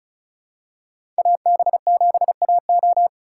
Morse code, heard as a keyed tone, about 36 wpm, 700 Hz. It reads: AB7AO
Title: USA Call Signs - 5 Characters - 3x Character Spacing - 35wpm